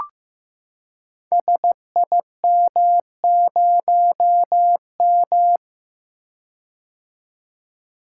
Morse code, heard as a keyed tone, about 15 wpm, 700 Hz. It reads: SIM0M
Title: Code Groups - 5 Letter:Numbers - 15wpm